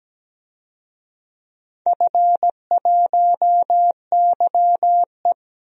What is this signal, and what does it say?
Morse code, heard as a keyed tone, about 17 wpm, 700 Hz. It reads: F1YE
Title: DXCC-100 Call Signs - 4 Characters - Repeated 3x - 17wpm